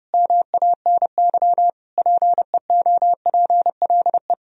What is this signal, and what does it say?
Morse code, heard as a keyed tone, about 30 wpm, 700 Hz. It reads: MANY PEOPLE
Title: Sentences from Top 1000 Words - Review Part 1 - 30wpm